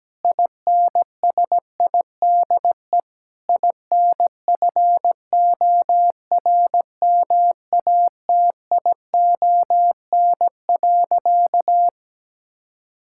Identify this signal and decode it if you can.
Morse code, heard as a keyed tone, about 17 wpm, 700 Hz. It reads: INSIDE INFORMATION.
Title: Sentences from Top 1000 Words - Review Part 6 - 17wpm